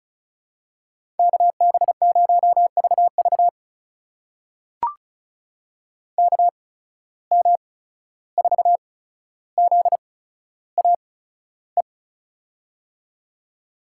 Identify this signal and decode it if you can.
Morse code, heard as a keyed tone, about 36 wpm, spaced as for 14 wpm, 700 Hz. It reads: KB0VV KM4ZAE
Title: USA Call Signs - 8x Character Spacing - 35wpm